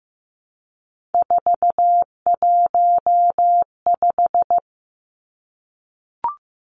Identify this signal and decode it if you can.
Morse code, heard as a keyed tone, about 15 wpm, 700 Hz. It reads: 415
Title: Numbers - 3-Digit 15wpm